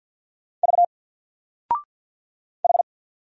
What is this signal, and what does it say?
Morse code, heard as a keyed tone, about 50 wpm, 700 Hz. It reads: V H
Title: Single Character (Letter:Number:Punctuation:Prosign) 50wpm